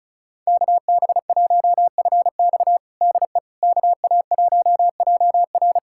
Morse code, heard as a keyed tone, about 36 wpm, 700 Hz. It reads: KB1FX DE KA1JR
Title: USA Callsign DE Callsign - 5 Characters - 35wpm